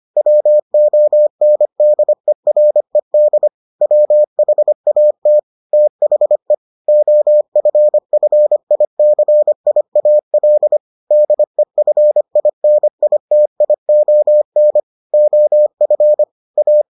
Morse code, heard as a keyed tone, about 25 wpm, 600 Hz. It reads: WONDERED WHAT THE OFFICIAL DEFINITION OF A